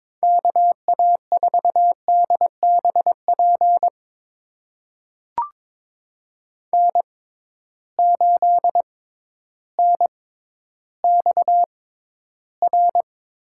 Morse code, heard as a keyed tone, about 21 wpm, spaced as for 12 wpm, 700 Hz. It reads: KA4DBP N8NXR
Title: USA Call Signs - 6x Character Spacing - 22wpm